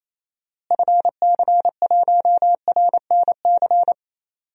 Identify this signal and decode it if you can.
Morse code, heard as a keyed tone, about 28 wpm, 700 Hz. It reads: FC1RNC
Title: DXCC-100 Call Signs - 6 Characters - Repeated 3x - 28wpm